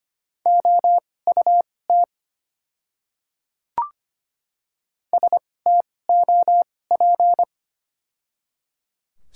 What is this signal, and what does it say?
Morse code, heard as a keyed tone, about 25 wpm, spaced as for 18 wpm, 700 Hz. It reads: OUT STOP
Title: 2x Character Spacing - Top 500 Words - 25wpm